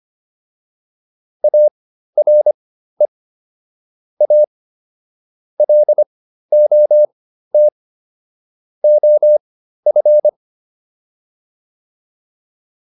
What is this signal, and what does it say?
Morse code, heard as a keyed tone, about 25 wpm, spaced as for 13 wpm, 600 Hz. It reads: ARE A LOT OF